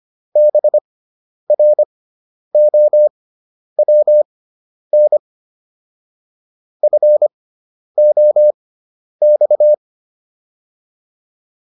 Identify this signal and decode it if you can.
Morse code, heard as a keyed tone, about 25 wpm, spaced as for 10 wpm, 600 Hz. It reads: BROWN FOX